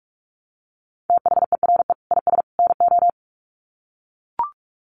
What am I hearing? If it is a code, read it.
Morse code, heard as a keyed tone, about 44 wpm, 700 Hz. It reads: THERE IS NO
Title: Sets of 3 Words 45wpm